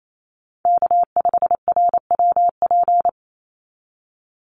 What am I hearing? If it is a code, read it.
Morse code, heard as a keyed tone, about 28 wpm, 700 Hz. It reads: K5RWP
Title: NA SOTA Chaser Callsigns 28wpm